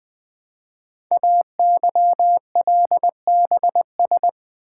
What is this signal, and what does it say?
Morse code, heard as a keyed tone, about 20 wpm, 700 Hz. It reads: AYLBS
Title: Code Groups - 5 Letter:Numbers - 20wpm